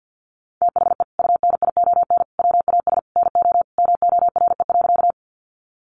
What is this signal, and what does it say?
Morse code, heard as a keyed tone, about 50 wpm, 700 Hz. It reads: THE UNION WAS NO MORE.
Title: Sentences from Top 1400 Words 50wpm